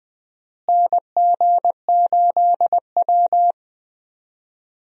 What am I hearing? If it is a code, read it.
Morse code, heard as a keyed tone, about 20 wpm, 700 Hz. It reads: NG8W
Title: USA Call Signs - 4 Characters - Repeated 2x - 20wpm